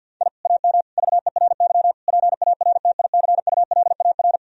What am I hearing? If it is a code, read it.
Morse code, heard as a keyed tone, about 50 wpm, 700 Hz. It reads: I AM VERY PARTICULAR
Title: Sentences from Top 1000 Words - Review Part 4 - 50wpm